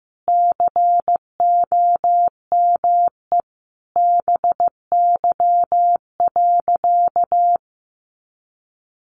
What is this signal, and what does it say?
Morse code, heard as a keyed tone, about 15 wpm, 700 Hz. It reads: COME BY.
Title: Sentences from Top 1500 Words 15wpm